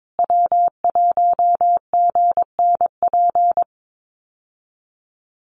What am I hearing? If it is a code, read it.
Morse code, heard as a keyed tone, about 22 wpm, 700 Hz. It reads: W1GNP
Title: USA Call Signs Repeated 3x 22wpm